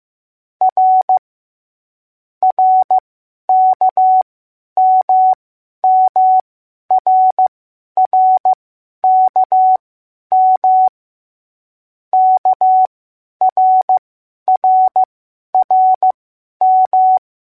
Morse code, heard as a keyed tone, about 15 wpm, spaced as for 10 wpm, 750 Hz. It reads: R RKMMRRKM KRRRM